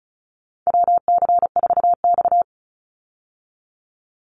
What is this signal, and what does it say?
Morse code, heard as a keyed tone, about 34 wpm, 700 Hz. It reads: WC4X
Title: Call Signs Repeated 2x - 35wpm